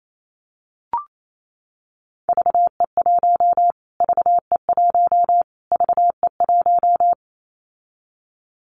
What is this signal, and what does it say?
Morse code, heard as a keyed tone, about 28 wpm, 700 Hz. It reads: VE1 VE1 VE1
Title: Callsign Prefixes for Top 25 DXCC - Repeated 3x - 28wpm